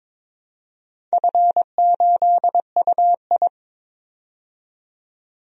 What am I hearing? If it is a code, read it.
Morse code, heard as a keyed tone, about 22 wpm, 690 Hz. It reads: F8UI
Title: DXCC-100 Call Signs - 4 Characters - 22wpm